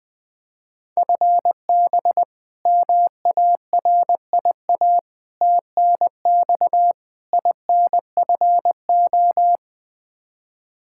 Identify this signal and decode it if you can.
Morse code, heard as a keyed tone, about 20 wpm, 700 Hz. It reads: FB MARIA TNX INFO
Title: K1USN SST QSO Practice 20wpm